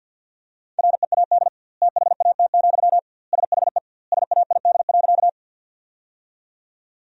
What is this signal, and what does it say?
Morse code, heard as a keyed tone, about 50 wpm, 700 Hz. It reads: READ THAT, SHE SAID.
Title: 2x Word-Spacing - Sentences from Top 200 Words - 50wpm